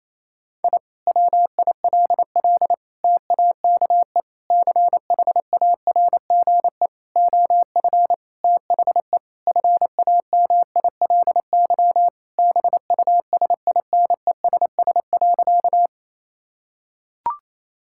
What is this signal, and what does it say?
Morse code, heard as a keyed tone, about 28 wpm, 700 Hz. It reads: I WILL TAKE CHARGE OF THE FAMILY BUSINESS.
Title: Sentences from Top 1000 Words - Review Part 6 - 28wpm